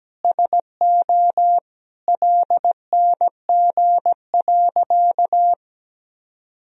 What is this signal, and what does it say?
Morse code, heard as a keyed tone, about 17 wpm, 700 Hz. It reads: SO LNG.
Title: Sentences from Top 100 Words and Abbreviations 17wpm